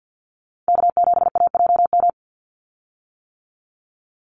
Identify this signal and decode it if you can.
Morse code, heard as a keyed tone, about 50 wpm, 700 Hz. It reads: K7AJM
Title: Call Signs - 50wpm